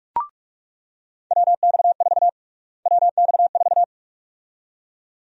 Morse code, heard as a keyed tone, about 41 wpm, 700 Hz. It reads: WX4 WX4
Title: USA Call Sign Prefixes - Repeated 2x - 45wpm